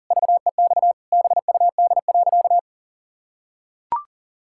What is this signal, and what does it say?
Morse code, heard as a keyed tone, about 40 wpm, 700 Hz. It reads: VEX BUD.
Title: Pangram Challenge v2 - 40wpm